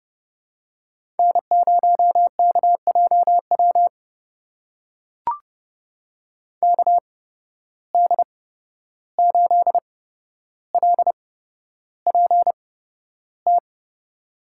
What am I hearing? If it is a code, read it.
Morse code, heard as a keyed tone, about 30 wpm, spaced as for 12 wpm, 700 Hz. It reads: N0KJW KD8LPT
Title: USA Call Signs - 8x Character Spacing - 30wpm